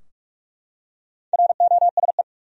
Morse code, heard as a keyed tone, about 42 wpm, 700 Hz. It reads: ROSE
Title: USA Common Names - Max 5 Letters - 45wpm